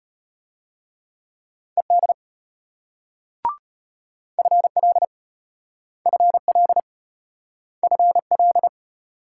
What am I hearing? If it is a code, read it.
Morse code, heard as a keyed tone, about 34 wpm, 700 Hz. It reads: ED FL FL FL